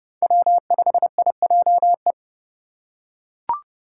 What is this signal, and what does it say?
Morse code, heard as a keyed tone, about 28 wpm, 700 Hz. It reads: W5IJE